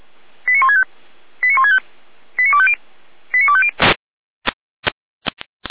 Five-tone selective call (ZVEI), heard as quick runs of five tones, tones of about 70 ms. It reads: 89165 89165 89250 89250
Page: audio.f85360.feuerwehr-orlamuende.org